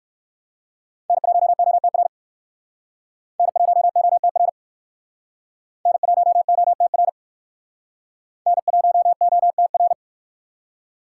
Morse code, heard as a keyed tone, about 58 wpm, 700 Hz. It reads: N1OTR N1OTR N1OTR N1OTR